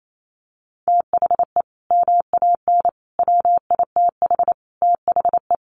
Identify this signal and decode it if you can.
Morse code, heard as a keyed tone, about 28 wpm, 700 Hz. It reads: THE MAN WITH THE
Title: Sentences from Top 1500 Words 28wpm